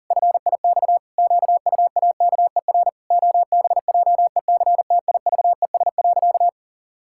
Morse code, heard as a keyed tone, about 40 wpm, 700 Hz. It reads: FIX QUAKER OBJECTIVES.
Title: Pangram Challenge v2 - 40wpm